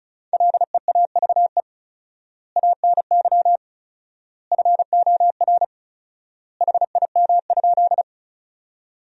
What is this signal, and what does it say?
Morse code, heard as a keyed tone, about 36 wpm, 700 Hz. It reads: LEAVE ANY FOR HIM?
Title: 4x Word-Spacing - Sentences from Top 400 Words - 35wpm